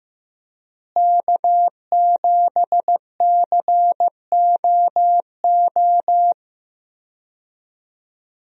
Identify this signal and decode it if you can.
Morse code, heard as a keyed tone, about 15 wpm, 700 Hz. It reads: K7COO
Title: USA Call Signs - 5 Characters - Repeated 3x - 15wpm